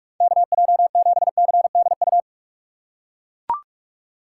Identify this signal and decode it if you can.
Morse code, heard as a keyed tone, about 44 wpm, 700 Hz. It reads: KJ7CDU
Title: USA Call Signs - 4x Character Spacing - 45wpm